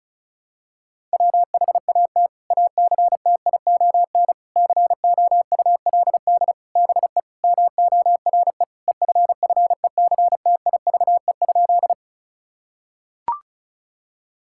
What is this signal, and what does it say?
Morse code, heard as a keyed tone, about 34 wpm, 700 Hz. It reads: WHAT ACTION COULD BE MORE EFFECTIVE?
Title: Sentences from Top 1000 Words - Review Part 6 - 35wpm